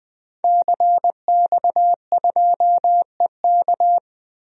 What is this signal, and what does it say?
Morse code, heard as a keyed tone, about 20 wpm, 700 Hz. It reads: CX2EK
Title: DXCC-100 Call Signs - 5 Characters - 20wpm